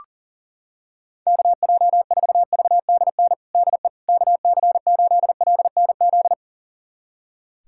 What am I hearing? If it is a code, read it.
Morse code, heard as a keyed tone, about 40 wpm, 700 Hz. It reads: KJ4VDN DE KC8LNZ
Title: USA Callsign DE Callsign - 6 Characters - 40wpm